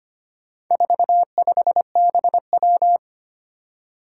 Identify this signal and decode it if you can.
Morse code, heard as a keyed tone, about 25 wpm, 700 Hz. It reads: 45BW
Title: Four Challenging Letter-Number Combinations 25wpm